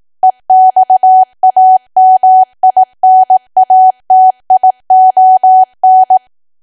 Morse code, heard as a keyed tone, about 18 wpm, 750 Hz. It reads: EXAMINATION
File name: cwa / W1AW-Level2 / 18WPM